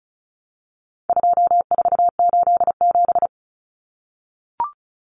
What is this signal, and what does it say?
Morse code, heard as a keyed tone, about 34 wpm, 700 Hz. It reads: 2487
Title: Numbers - 4-Digit 35wpm